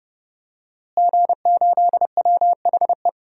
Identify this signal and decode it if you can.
Morse code, heard as a keyed tone, about 30 wpm, 700 Hz. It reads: G8WHE
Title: DXCC-100 Call Signs - 5 Characters - Repeated 2x - 30wpm